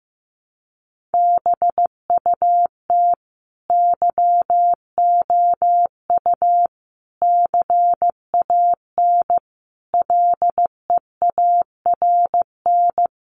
Morse code, heard as a keyed tone, about 15 wpm, 700 Hz. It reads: BUT YOU CAN LEARN
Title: Sentences from Top 1500 Words 15wpm